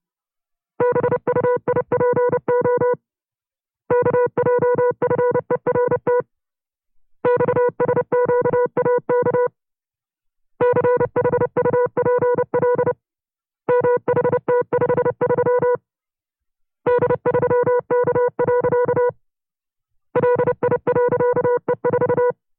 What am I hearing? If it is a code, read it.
Morse code, heard as a keyed tone, about 29 wpm, 490 Hz. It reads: BUIPO KJFERT XSQAK CHUPL MHT53 D3K. LI.E4
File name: FreeFiles / CW-practice / 25WPM-side2